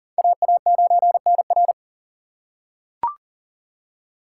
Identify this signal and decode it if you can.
Morse code, heard as a keyed tone, about 40 wpm, 700 Hz. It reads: AA9NR
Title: USA Call Signs - 3x Character Spacing - 40wpm